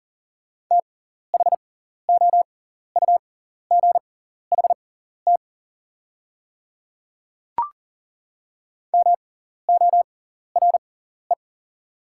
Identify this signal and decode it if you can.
Morse code, heard as a keyed tone, about 40 wpm, spaced as for 14 wpm, 700 Hz. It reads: THOUGHT MORE
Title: 6x Character Spacing - Top 500 Words - 40wpm